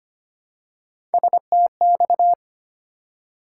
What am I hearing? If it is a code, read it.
Morse code, heard as a keyed tone, about 25 wpm, 700 Hz. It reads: STX
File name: practice / 25wpm / ARRL Sections v2 - 25wpm